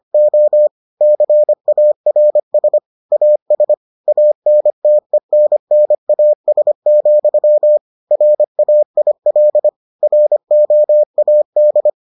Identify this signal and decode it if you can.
Morse code, heard as a keyed tone, about 25 wpm, 600 Hz. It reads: O CARS AS ANTENNAS, RAIL ROAD